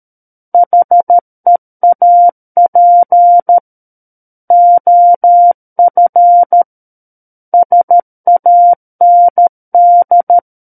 Morse code, heard as a keyed tone, about 13 wpm, 700 Hz. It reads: HEAP OF SAND
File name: Book One / 04-Chapter 4 - 13 WPM